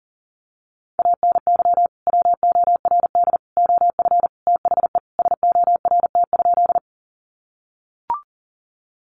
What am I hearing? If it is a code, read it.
Morse code, heard as a keyed tone, about 40 wpm, 700 Hz. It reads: ANY WORD OF THE SORT?
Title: Sentences from Top 1300 Words 40wpm